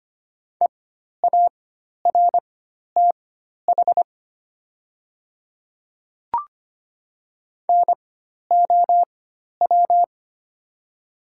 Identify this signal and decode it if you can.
Morse code, heard as a keyed tone, about 25 wpm, spaced as for 12 wpm, 700 Hz. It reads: EARTH NOW
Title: 4x Character Spacing - Top 200 Words - 25wpm